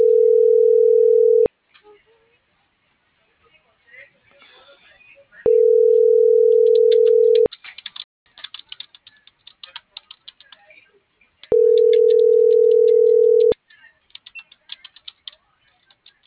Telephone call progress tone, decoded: ringback tone

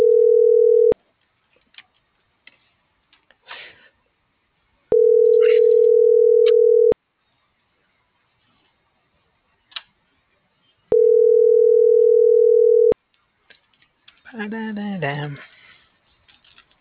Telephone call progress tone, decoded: ringback tone

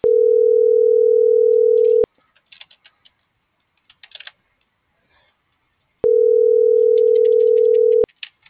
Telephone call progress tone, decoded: ringback tone

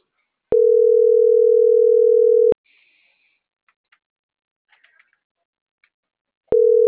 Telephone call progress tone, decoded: ringback tone

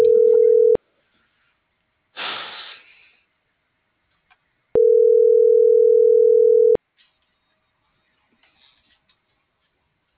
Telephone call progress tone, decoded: ringback tone

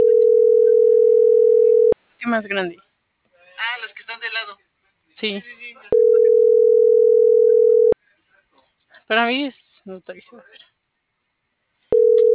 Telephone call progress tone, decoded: ringback tone

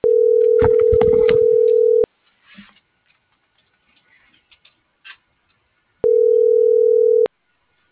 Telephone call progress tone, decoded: ringback tone

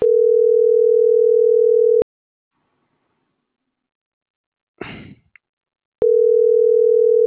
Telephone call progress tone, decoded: ringback tone

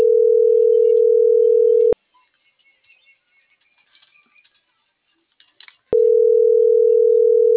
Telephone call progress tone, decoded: ringback tone

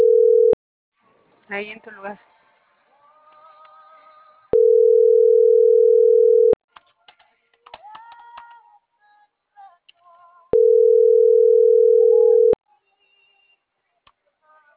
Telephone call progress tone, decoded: ringback tone

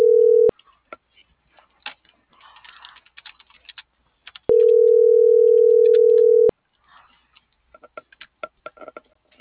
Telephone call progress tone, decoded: ringback tone

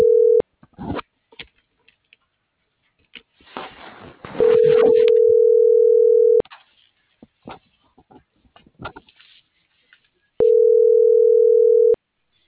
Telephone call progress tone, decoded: ringback tone

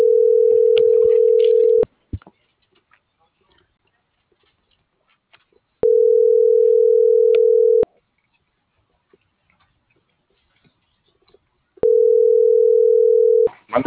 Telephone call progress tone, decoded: ringback tone

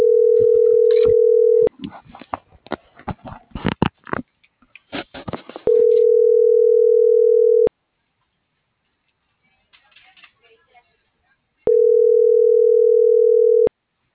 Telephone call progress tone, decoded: ringback tone

